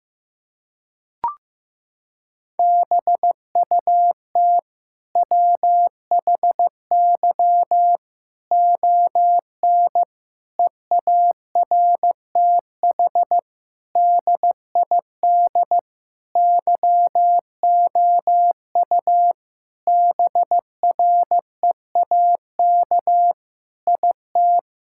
Morse code, heard as a keyed tone, about 15 wpm, 700 Hz. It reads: BUT WHY ON EARTH DID YOU BREAK IT
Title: Sentences from Top 1000 Words - Review Part 6 - 15wpm